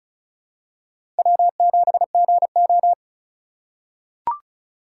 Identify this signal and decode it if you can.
Morse code, heard as a keyed tone, about 36 wpm, 700 Hz. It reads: W7GO
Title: USA Call Signs - 3x Character Spacing - 35wpm